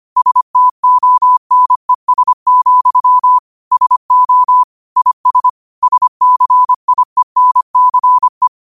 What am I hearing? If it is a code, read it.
Morse code, heard as a keyed tone, about 25 wpm, 1000 Hz. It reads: STONES, SO IS SCIENCE
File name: qotdfiles / 25wpm